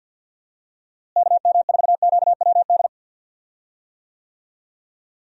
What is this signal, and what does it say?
Morse code, heard as a keyed tone, about 50 wpm, 700 Hz. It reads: KM4QWD
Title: USA Call Signs - 6 Characters - Repeated 3x - 50wpm